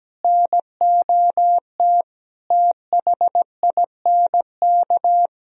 Morse code, heard as a keyed tone, about 17 wpm, 700 Hz. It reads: NOT THINK
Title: Sentences from Top 1000 Words - Review Part 1 - 17wpm